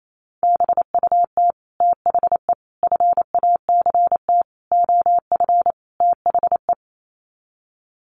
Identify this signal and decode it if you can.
Morse code, heard as a keyed tone, about 28 wpm, 700 Hz. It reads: BUT THE FACT OF THE
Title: Sets of 5 Words Encore2 28wpm